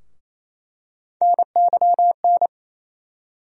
Morse code, heard as a keyed tone, about 28 wpm, 700 Hz. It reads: NYN
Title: USA Call Sign Postfixes - 3 Letters 28wpm